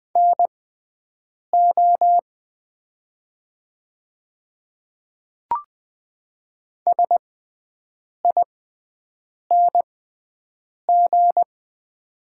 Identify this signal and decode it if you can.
Morse code, heard as a keyed tone, about 20 wpm, spaced as for 7 wpm, 700 Hz. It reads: NO SING
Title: 6x Character Spacing - Top 500 Words - 20wpm